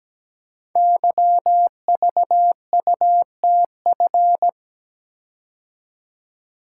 Morse code, heard as a keyed tone, about 17 wpm, 690 Hz. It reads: YVUTF
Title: Code Groups - 5 Letter:Numbers - 17wpm